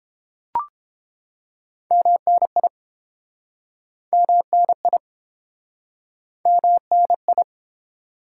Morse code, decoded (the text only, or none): MNI MNI MNI